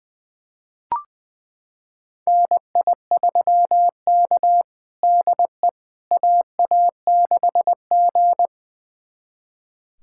Morse code, heard as a keyed tone, about 20 wpm, 700 Hz. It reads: NI3K DE AA6G
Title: USA Callsign DE Callsign - 4 Characters - 20wpm